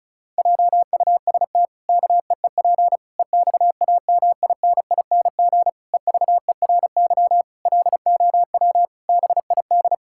Morse code, heard as a keyed tone, about 34 wpm, 700 Hz. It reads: JUST KEEP EXAMINING EVERY LOW BID
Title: Pangram Challenge v2 - 35wpm